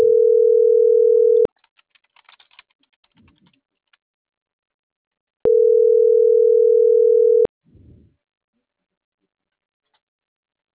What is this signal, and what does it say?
Ringback tone (North America), 440 + 480 Hz.